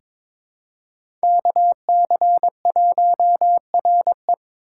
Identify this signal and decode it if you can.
Morse code, heard as a keyed tone, about 22 wpm, 700 Hz. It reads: KC1RE
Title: USA Call Signs - 6x Character Spacing - 22wpm